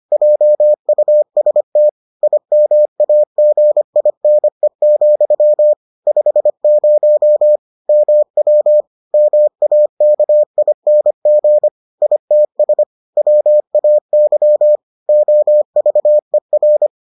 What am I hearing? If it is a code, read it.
Morse code, heard as a keyed tone, about 25 wpm, 600 Hz. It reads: JUST IMAGINE, 50 MW MAKING ITS WAY OVER